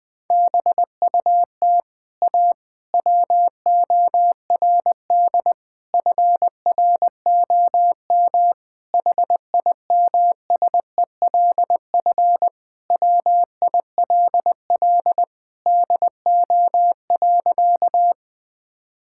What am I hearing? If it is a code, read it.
Morse code, heard as a keyed tone, about 21 wpm, 700 Hz. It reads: BUT A WORD FROM HIMSELF WILL DO.
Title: Sentences from Top 1600 Words 20wpm